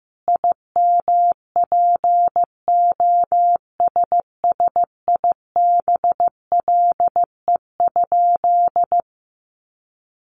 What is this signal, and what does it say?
Morse code, heard as a keyed tone, about 15 wpm, 700 Hz. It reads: IMPOSSIBLE?
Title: Sentences from Top 1100 Words 15wpm